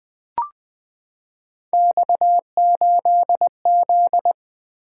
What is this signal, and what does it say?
Morse code, heard as a keyed tone, about 20 wpm, 700 Hz. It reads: X8Z
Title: Three Letter-Number Combinations - 20wpm